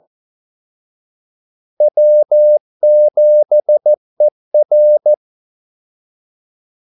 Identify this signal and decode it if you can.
Morse code, heard as a keyed tone, about 14 wpm, 600 Hz. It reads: W7ER
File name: Callsigns-14wpm